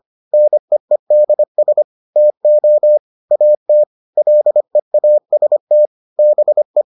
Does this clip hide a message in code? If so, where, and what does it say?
Yes: Morse code from 0.3 to 6.8 s, heard as a keyed tone, about 25 wpm, 600 Hz. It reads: NEEDS TO AT LEAST BE